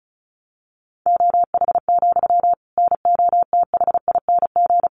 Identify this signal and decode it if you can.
Morse code, heard as a keyed tone, about 34 wpm, 700 Hz. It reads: OH, NOTHING
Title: Sentences from Top 700 Words 35wpm